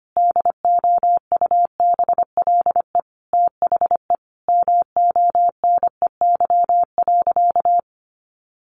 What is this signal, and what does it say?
Morse code, heard as a keyed tone, about 25 wpm, 700 Hz. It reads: DOUBLE THE MONEY.
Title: Sentences from Top 1400 Words 25wpm